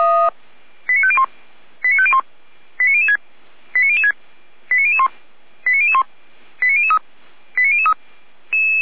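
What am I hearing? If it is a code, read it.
Five-tone selective call (ZVEI), heard as quick runs of five tones, tones of about 70 ms. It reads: 89501 89501 89006 89006 89001 89001 89003 89003
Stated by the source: ZVEI five-tone selective call received on an SDR, from the Feuerwehr Orlamünde fire brigade audio server.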